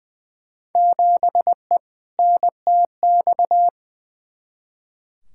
Morse code, heard as a keyed tone, about 20 wpm, 700 Hz. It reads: 7E NTX
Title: Field Day Exchanges Encore - 20wpm